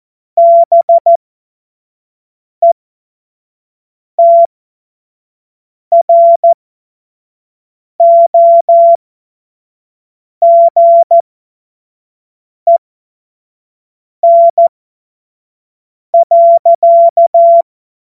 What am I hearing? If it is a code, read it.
Morse code, heard as a keyed tone, about 14 wpm, spaced as for 5 wpm, 680 Hz. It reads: BETROGEN.
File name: Morsekurs / Witze / Witz 22. Februar 2022 25 F70